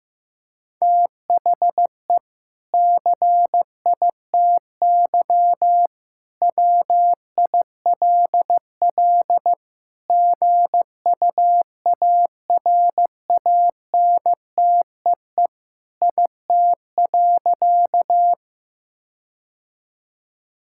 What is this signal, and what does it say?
Morse code, heard as a keyed tone, about 15 wpm, 700 Hz. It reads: THE CITY WILL GUARANTEE IT.
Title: Sentences from Top 1600 Words 15wpm